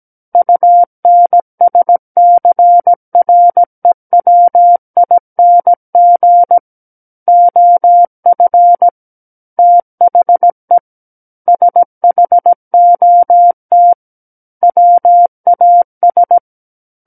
Morse code, heard as a keyed tone, about 17 wpm, 700 Hz. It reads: UNSCREWING OF THE SHOT WAS
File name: Book One / 08-Chapter 8 - 17 WPM